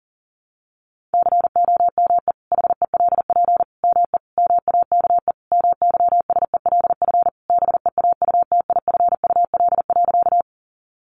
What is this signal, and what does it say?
Morse code, heard as a keyed tone, about 37 wpm, 700 Hz. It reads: COME HELP ME MAKE MYSELF BEAUTIFUL.
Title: Sentences from Top 1300 Words 40wpm